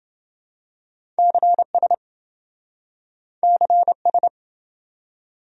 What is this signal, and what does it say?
Morse code, heard as a keyed tone, about 27 wpm, 700 Hz. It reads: CS CS